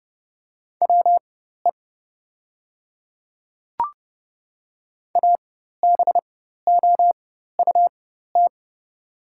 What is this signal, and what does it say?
Morse code, heard as a keyed tone, about 30 wpm, spaced as for 14 wpm, 700 Hz. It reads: WE ABOUT
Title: 4x Character Spacing - Top 100 Words - 30wpm